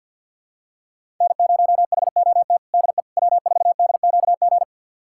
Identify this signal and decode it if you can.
Morse code, heard as a keyed tone, about 49 wpm, 700 Hz. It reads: N0HOT DE W4DQG
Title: USA Callsign DE Callsign - 5 Characters - 50wpm